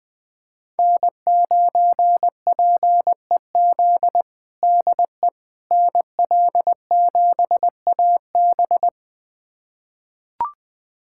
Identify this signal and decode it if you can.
Morse code, heard as a keyed tone, about 20 wpm, 700 Hz. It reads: N9PEZ DE NL7AB
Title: USA Callsign DE Callsign - 5 Characters - 20wpm